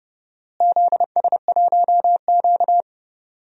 Morse code, heard as a keyed tone, about 30 wpm, 700 Hz. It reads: ZS1Q
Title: DXCC-100 Call Signs - 4 Characters - Repeated 2x - 30wpm